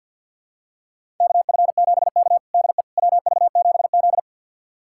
Morse code, heard as a keyed tone, about 51 wpm, 700 Hz. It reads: KF7K DE WV7Z
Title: USA Callsign DE Callsign - 4 Characters - 50wpm